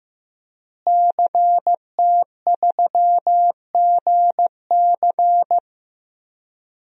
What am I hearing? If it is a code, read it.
Morse code, heard as a keyed tone, about 15 wpm, 700 Hz. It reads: CT3GC
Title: DXCC-100 Call Signs - 5 Characters - Repeated 3x - 15wpm